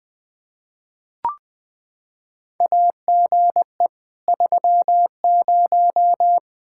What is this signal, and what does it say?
Morse code, heard as a keyed tone, about 20 wpm, 700 Hz. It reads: AGE 30
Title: Practice QSO - Exchange 3 - More Info - 20wpm